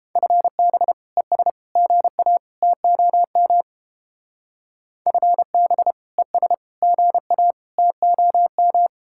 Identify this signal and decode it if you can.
Morse code, heard as a keyed tone, about 31 wpm, 700 Hz. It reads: FB ES GA TOM FB ES GA TOM